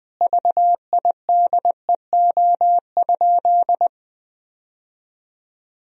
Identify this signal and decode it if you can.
Morse code, heard as a keyed tone, about 21 wpm, 700 Hz. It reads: VIDEO?
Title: Sentences from Top 1000 Words - Review Part 4 - 20wpm